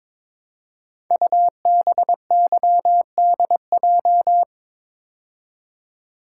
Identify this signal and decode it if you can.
Morse code, heard as a keyed tone, about 22 wpm, 700 Hz. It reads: UBYDJ
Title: Code Groups - 5 Letters - 22wpm